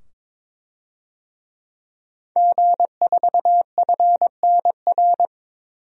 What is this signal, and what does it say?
Morse code, heard as a keyed tone, about 22 wpm, 700 Hz. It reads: G4FNR
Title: DXCC-100 Call Signs - 5 Characters - 22wpm